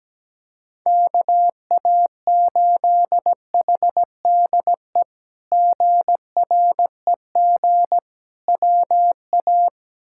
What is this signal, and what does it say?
Morse code, heard as a keyed tone, about 17 wpm, 700 Hz. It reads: KA8HDE GREG WA
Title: K1USN SST QSO Practice - 17wpm